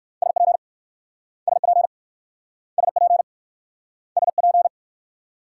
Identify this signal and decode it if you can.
Morse code, heard as a keyed tone, about 60 wpm, 700 Hz. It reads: SP SP SP SP